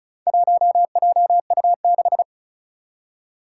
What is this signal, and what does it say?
Morse code, heard as a keyed tone, about 36 wpm, 700 Hz. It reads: 1JU6
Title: Four Challenging Letter-Number Combinations 35wpm